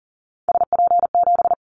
Morse code, heard as a keyed tone, about 40 wpm, 700 Hz. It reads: SP7
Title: Callsign Prefixes for Top 25 DXCC - Repeated 3x - 40wpm